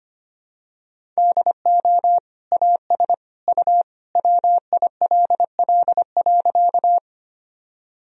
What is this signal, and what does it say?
Morse code, heard as a keyed tone, about 25 wpm, 700 Hz. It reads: DO AS U WILL.
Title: Sentences from Top 100 Words and Abbreviations 25wpm